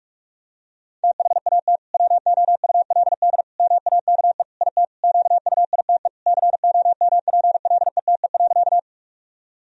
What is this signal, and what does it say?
Morse code, heard as a keyed tone, about 42 wpm, 700 Hz. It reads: THAT WOULD MAKE IT QUITE COMPLETE.
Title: Sentences from Top 1000 Words - Review Part 2 - 45wpm